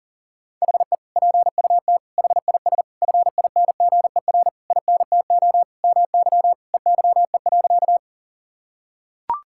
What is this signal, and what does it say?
Morse code, heard as a keyed tone, about 40 wpm, 700 Hz. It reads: HE PUT HIS FINGER INTO MY EYE.
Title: Sentences from Top 1600 Words 40wpm